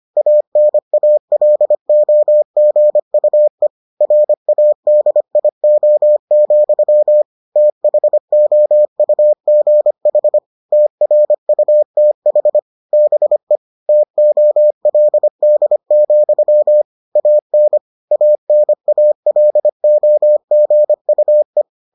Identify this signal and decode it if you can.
Morse code, heard as a keyed tone, about 25 wpm, 600 Hz. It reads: ANALOGUE RADIO, THOUGH TRUTH BE TOLD, AN ANALOGUE